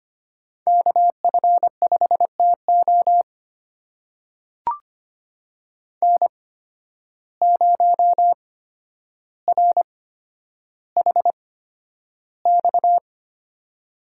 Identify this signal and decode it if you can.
Morse code, heard as a keyed tone, about 25 wpm, spaced as for 11 wpm, 700 Hz. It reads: KF5TO N0RHX
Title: USA Call Signs - 5 Characters - 8x Character Spacing - 25wpm